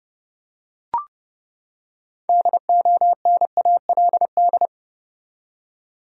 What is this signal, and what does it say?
Morse code, heard as a keyed tone, about 30 wpm, 700 Hz. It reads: DONALD
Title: USA Common Names - 30wpm